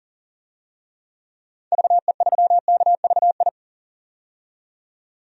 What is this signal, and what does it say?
Morse code, heard as a keyed tone, about 40 wpm, 700 Hz. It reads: VE3KVI
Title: DXCC-100 Call Signs - 6 Characters - 40wpm